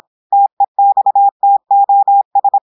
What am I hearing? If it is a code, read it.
Morse code, heard as a keyed tone, about 26 wpm, 800 Hz. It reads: TEXTOS
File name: audios / portugais / portugais texte en clair 1 vitesse 26 mots minutes